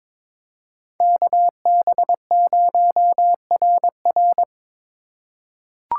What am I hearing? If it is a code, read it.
Morse code, heard as a keyed tone, about 22 wpm, 700 Hz. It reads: KB0RR
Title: USA Call Signs - 5 Characters - 4x Character Spacing - 22wpm